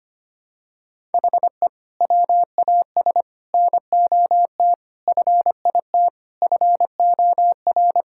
Morse code, heard as a keyed tone, about 25 wpm, 700 Hz. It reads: HE WAS NOT FIT FOR
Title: Sentences from Top 1000 Words - Review Part 5 - 25wpm